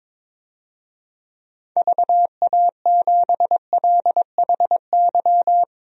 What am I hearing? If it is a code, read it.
Morse code, heard as a keyed tone, about 22 wpm, 700 Hz. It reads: VA7LHY
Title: DXCC-100 Call Signs - 6 Characters - Repeated 3x - 22wpm